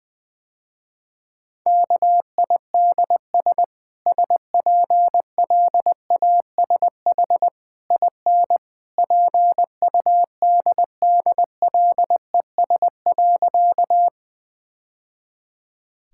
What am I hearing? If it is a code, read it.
Morse code, heard as a keyed tone, about 21 wpm, 700 Hz. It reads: KIDS SPLASH IN PUDDLES.
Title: Easy Sentences with Hints - 20wpm